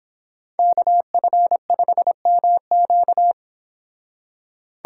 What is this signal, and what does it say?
Morse code, heard as a keyed tone, about 26 wpm, 700 Hz. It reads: KF5MQ